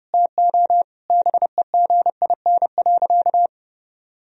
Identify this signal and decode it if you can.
Morse code, heard as a keyed tone, about 32 wpm, 700 Hz. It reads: TO BEGIN.
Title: Sentences from Top 1000 Words - Review Part 1 - 30wpm